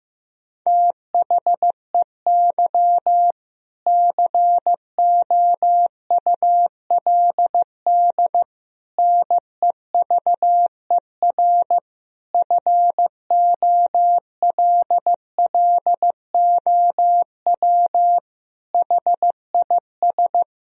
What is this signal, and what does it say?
Morse code, heard as a keyed tone, about 15 wpm, 700 Hz. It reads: THEY COULD NEVER FOLLOW HIS
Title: Sentences from Top 1000 Words - Review Part 4 - 15wpm